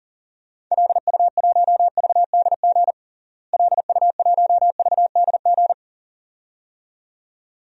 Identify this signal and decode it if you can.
Morse code, heard as a keyed tone, about 40 wpm, 700 Hz. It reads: LU1VDG LU1VDG
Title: DXCC-100 Call Signs - 6 Characters - Repeated 2x - 40wpm